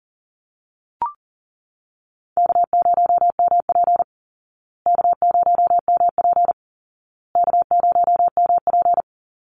Morse code, heard as a keyed tone, about 40 wpm, 700 Hz. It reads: K0MP K0MP K0MP
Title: Call Signs Repeated 2x - 40wpm